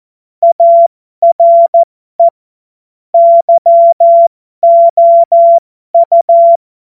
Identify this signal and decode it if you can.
Morse code, heard as a keyed tone, about 14 wpm, spaced as for 12 wpm, 680 Hz. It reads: ARE YOU
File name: Morsekurs / Witze